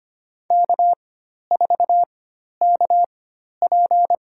Morse code, heard as a keyed tone, about 25 wpm, spaced as for 12 wpm, 700 Hz. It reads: K4KP
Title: USA Call Signs - 4 Characters - 4x Character Spacing - 25wpm